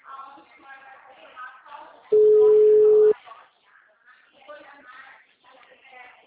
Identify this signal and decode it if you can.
Ringback tone (Europe, 425 Hz), with speech.